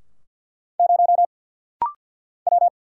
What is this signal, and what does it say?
Morse code, heard as a keyed tone, about 50 wpm, 700 Hz. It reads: 0 W
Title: Single Letter-Number Alt - Rapid-Fire 50wpm